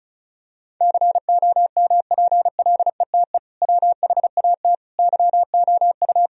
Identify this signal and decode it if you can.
Morse code, heard as a keyed tone, about 34 wpm, 700 Hz. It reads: COMPLETE WHAT YOU
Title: Sentences from Top 1000 Words - Review Part 2 - 35wpm